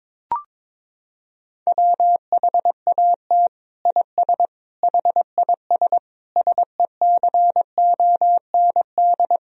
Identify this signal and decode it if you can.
Morse code, heard as a keyed tone, about 22 wpm, 700 Hz. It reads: WHAT IS HIS SECOND
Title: Sentences from Top 1000 Words - Review Part 1 - 22wpm